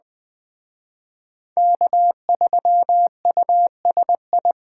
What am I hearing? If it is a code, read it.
Morse code, heard as a keyed tone, about 20 wpm, 700 Hz. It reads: K3USI
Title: Speed Racing - USA 1x3 Call Signs 17wpm